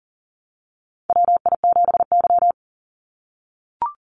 Morse code, heard as a keyed tone, about 38 wpm, 700 Hz. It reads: WI7Y
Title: USA Call Signs 40wpm